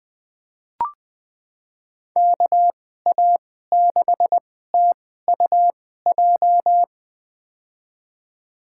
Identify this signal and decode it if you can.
Morse code, heard as a keyed tone, about 20 wpm, spaced as for 14 wpm, 700 Hz. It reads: KA6TUJ
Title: USA Call Signs - 2x Character Spacing - 20wpm